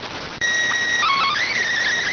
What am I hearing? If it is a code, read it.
SSTV (slow-scan television), PD 290.